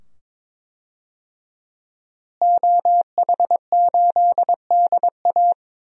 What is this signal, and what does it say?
Morse code, heard as a keyed tone, about 22 wpm, 700 Hz. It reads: OH8DA
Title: DXCC-100 Call Signs - 5 Characters - Repeated 2x - 22wpm